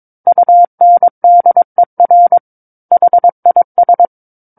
Morse code, heard as a keyed tone, about 22 wpm, 700 Hz. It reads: UNDER HIS
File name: 06-Chapter-06 22 WPM